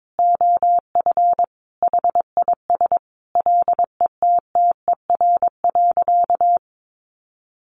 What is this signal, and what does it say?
Morse code, heard as a keyed tone, about 22 wpm, 700 Hz. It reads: OF HIS LETTER.
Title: Sentences from Top 1500 Words 22wpm